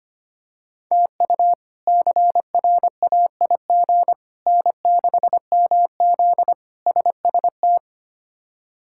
Morse code, heard as a keyed tone, about 25 wpm, 700 Hz. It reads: TU CRAIG N6MZ SST
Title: K1USN SST QSO Practice - 25wpm